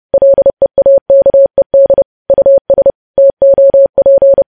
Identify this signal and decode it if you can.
Morse code, heard as a keyed tone, about 30 wpm, 550 Hz. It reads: LEAKED US TOP
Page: news headlines in morse code at 30 wpm